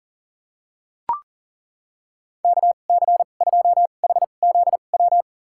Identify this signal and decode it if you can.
Morse code, heard as a keyed tone, about 37 wpm, spaced as for 28 wpm, 700 Hz. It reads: KC2HZW
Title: USA Call Signs - 2x Character Spacing - 40wpm